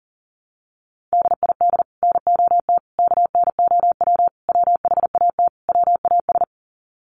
Morse code, heard as a keyed tone, about 40 wpm, 700 Hz. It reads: DID NOT KNOW WHAT WAS
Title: Sets of 5 Words Encore2 40wpm